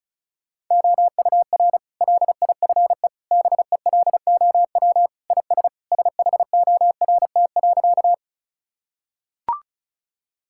Morse code, heard as a keyed tone, about 34 wpm, 700 Hz. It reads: OUR LIFE BELOW IS SHORT.
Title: Sentences from Top 1000 Words - Review Part 5 - 35wpm